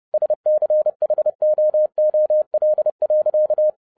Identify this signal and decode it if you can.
Morse code, heard as a keyed tone, about 30 wpm, 610 Hz. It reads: SCHOOL.